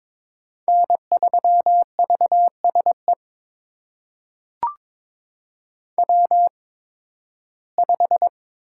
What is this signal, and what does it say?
Morse code, heard as a keyed tone, about 22 wpm, spaced as for 14 wpm, 700 Hz. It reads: N3VSE W5
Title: USA Call Signs - 5 Characters - 8x Character Spacing - 22wpm